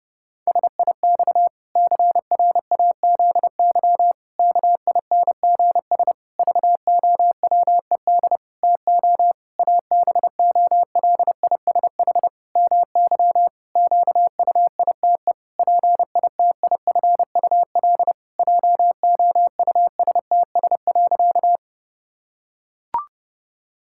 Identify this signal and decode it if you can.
Morse code, heard as a keyed tone, about 30 wpm, 700 Hz. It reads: SIX CRAZY KINGS VOWED TO ABOLISH MY QUITE PITIFUL JOUSTS.
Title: Pangram Challenge v2 - 30wpm